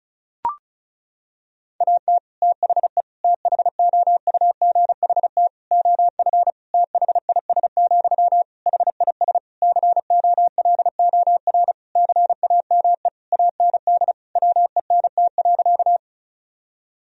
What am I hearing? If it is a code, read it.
Morse code, heard as a keyed tone, about 34 wpm, 700 Hz. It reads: AT THE THOUGHT OF THIS, HIS COLOR CAME AND WENT.
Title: Sentences from Top 1000 Words - Review Part 2 - 35wpm